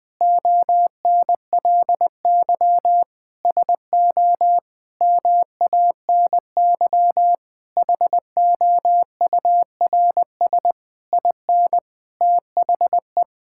Morse code, heard as a keyed tone, about 20 wpm, 700 Hz. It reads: ONLY SO MANY HOURS IN THE
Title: Sets of 9 Words - 20wpm